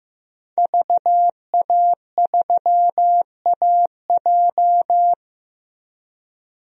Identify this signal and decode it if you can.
Morse code, heard as a keyed tone, about 15 wpm, 700 Hz. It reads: VA3AJ
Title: DXCC-100 Call Signs - 5 Characters - Repeated 3x - 15wpm